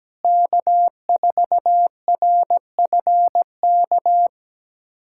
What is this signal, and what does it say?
Morse code, heard as a keyed tone, about 17 wpm, 700 Hz. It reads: K4RFK
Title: USA Callsign DE Callsign - 5 Characters - 17wpm